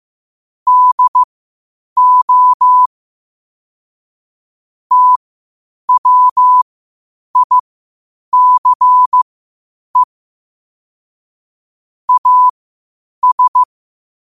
Morse code, heard as a keyed tone, about 15 wpm, spaced as for 8 wpm, 1000 Hz. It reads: DO TWICE AS
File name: qotdfiles / 8wpm